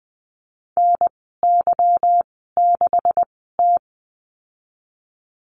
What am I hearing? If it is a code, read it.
Morse code, heard as a keyed tone, about 20 wpm, spaced as for 14 wpm, 700 Hz. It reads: NY6T
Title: 186 Lesson 30g - Letter b - Call Signs 20wpm